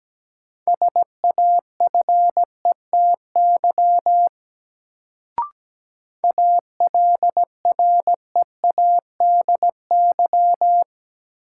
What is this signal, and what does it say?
Morse code, heard as a keyed tone, about 17 wpm, 700 Hz. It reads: SAFETY ALREADY
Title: Sentences from Top 700 Words - New Word Drill - 17wpm